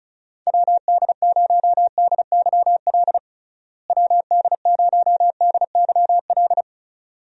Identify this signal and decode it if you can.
Morse code, heard as a keyed tone, about 34 wpm, 700 Hz. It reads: WD0DYL WD0DYL
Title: USA Call Signs - 6 Characters - Repeated 2x - 35wpm